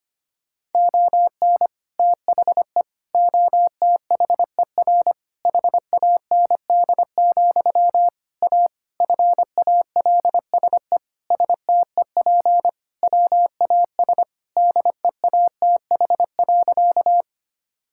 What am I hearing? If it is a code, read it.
Morse code, heard as a keyed tone, about 25 wpm, 700 Hz. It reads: ON THE OTHER HAND, A FALSE STEP WAS DEATH.
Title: Sentences from Top 1600 Words 25wpm